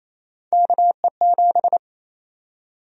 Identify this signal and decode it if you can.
Morse code, heard as a keyed tone, about 28 wpm, 700 Hz. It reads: KE7
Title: USA Call Sign Prefixes - 28wpm